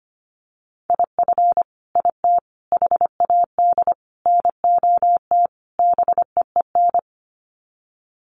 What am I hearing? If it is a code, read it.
Morse code, heard as a keyed tone, about 25 wpm, 690 Hz. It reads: IF IT HAD NOT BEEN
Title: Sets of 5 Words Encore 25wpm